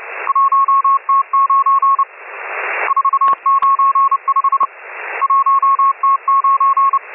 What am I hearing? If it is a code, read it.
Morse code, heard as a keyed tone, about 29 wpm, 1100 Hz, with radio noise. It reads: 1T9 495 1T9
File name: M12- -1010utc